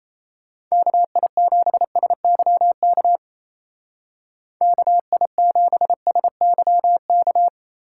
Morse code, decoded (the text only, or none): KI7SYK KI7SYK